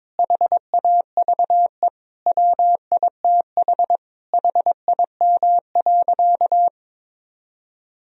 Morse code, heard as a keyed tone, about 22 wpm, 700 Hz. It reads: HAVE WITH HIM.
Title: Sentences from Top 1000 Words - Review Part 1 - 22wpm